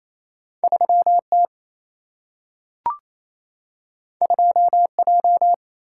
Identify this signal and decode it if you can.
Morse code, heard as a keyed tone, about 28 wpm, 700 Hz. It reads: 3 T 2 J
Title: Single-Letter Single-Number 28wpm